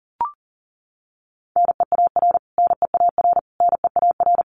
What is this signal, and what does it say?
Morse code, heard as a keyed tone, about 40 wpm, 700 Hz. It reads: NEAR NEAR NEAR
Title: Top 500 Words Repeated 40wpm